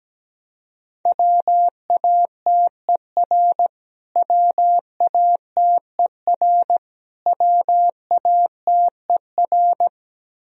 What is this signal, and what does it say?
Morse code, heard as a keyed tone, about 17 wpm, 700 Hz. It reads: WATER WATER WATER
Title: Top 100 Words Repeated 17wpm